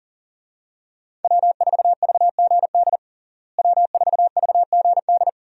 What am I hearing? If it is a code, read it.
Morse code, heard as a keyed tone, about 40 wpm, 700 Hz. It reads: W4VGD W4VGD
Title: USA Call Signs - 5 Characters - Repeated 3x - 40wpm